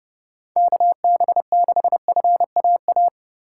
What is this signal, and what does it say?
Morse code, heard as a keyed tone, about 30 wpm, 700 Hz. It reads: KB6FAA
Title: USA Call Signs - 2x Character Spacing - 30wpm